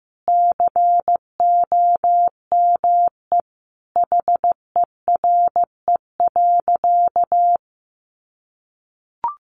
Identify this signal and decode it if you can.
Morse code, heard as a keyed tone, about 14 wpm, 700 Hz. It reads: COME HERE.
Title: Sentences from Top 1300 Words 15wpm